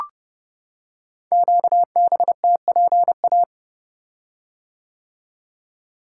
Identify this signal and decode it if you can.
Morse code, heard as a keyed tone, about 30 wpm, 700 Hz. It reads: QBTPA
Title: Code Groups - 5 Letters - 30wpm